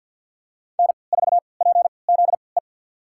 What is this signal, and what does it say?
Morse code, heard as a keyed tone, about 52 wpm, spaced as for 29 wpm, 700 Hz. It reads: N4PZE
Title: USA Call Signs - 5 Characters - 3x Character Spacing - 50wpm